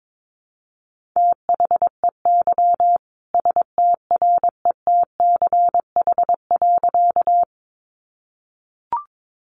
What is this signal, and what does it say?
Morse code, heard as a keyed tone, about 22 wpm, 700 Hz. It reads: THEY STRETCH.
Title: Easy Phrases 22wpm